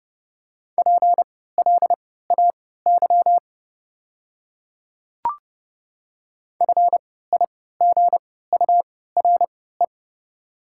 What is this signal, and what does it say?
Morse code, heard as a keyed tone, about 32 wpm, spaced as for 17 wpm, 690 Hz. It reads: PLAY FIGURE
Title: 3x Character Spacing - Top 500 Words - 30wpm